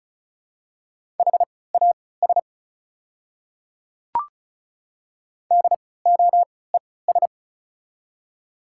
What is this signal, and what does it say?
Morse code, heard as a keyed tone, about 36 wpm, spaced as for 20 wpm, 700 Hz. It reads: HAS DOES
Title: 3x Character Spacing - Top 200 Words - 35wpm